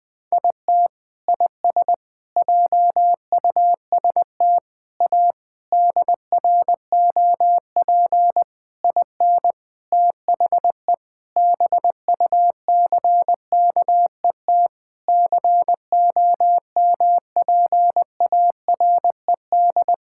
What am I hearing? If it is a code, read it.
Morse code, heard as a keyed tone, about 21 wpm, 700 Hz. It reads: IT IS JUST A DROP IN THE BUCKET COMPARED
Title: Sentences with 8-Word Phrases - 20wpm